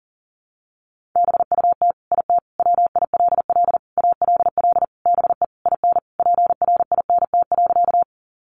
Morse code, heard as a keyed tone, about 40 wpm, 700 Hz. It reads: BUT IT WILL ALL BE IN PRINT.
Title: Sentences from Top 1400 Words 40wpm